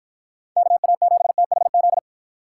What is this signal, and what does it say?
Morse code, heard as a keyed tone, about 53 wpm, 700 Hz. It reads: KA7THZ